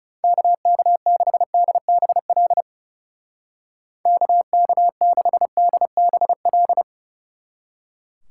Morse code, heard as a keyed tone, about 36 wpm, 700 Hz. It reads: KK6DBL KK6DBL
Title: Speed Racing - USA 2x3 Call Signs 30wpm